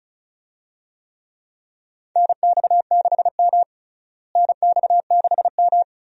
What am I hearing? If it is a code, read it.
Morse code, heard as a keyed tone, about 34 wpm, 700 Hz. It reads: NX6M NX6M
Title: USA Call Signs - 4 Characters - Repeated 2x - 35wpm